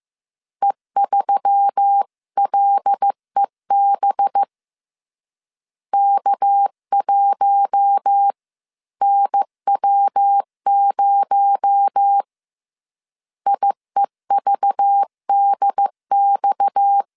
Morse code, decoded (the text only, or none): E3LEB K1NW0 IEVDX